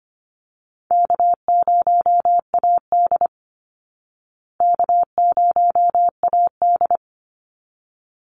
Morse code, heard as a keyed tone, about 25 wpm, 700 Hz. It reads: K0AD K0AD
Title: Easy Call Signs Repeated 2x - 25wpm